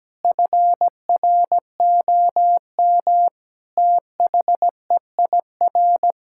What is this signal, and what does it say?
Morse code, heard as a keyed tone, about 17 wpm, 700 Hz. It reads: FROM THEIR
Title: Sentences from Top 1000 Words - Review Part 6 - 17wpm